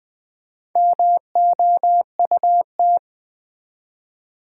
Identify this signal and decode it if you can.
Morse code, heard as a keyed tone, about 20 wpm, 700 Hz. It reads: MOUT
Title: Build A Word - 1 Letter At A Time - 3 to 5-Letter Words - 20wpm